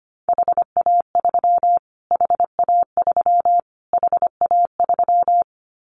Morse code, heard as a keyed tone, about 25 wpm, 700 Hz. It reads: HA3 HA3 HA3
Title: Callsign Prefixes for Top 25 DXCC - Repeated 3x - 25wpm